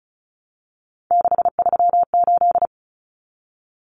Morse code, heard as a keyed tone, about 34 wpm, 700 Hz. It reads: 638
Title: Numbers - 3-Digit 35wpm